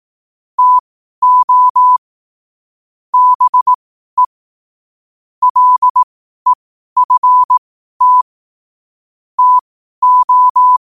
Morse code, heard as a keyed tone, about 18 wpm, spaced as for 12 wpm, 1000 Hz. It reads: TO BE LEFT TO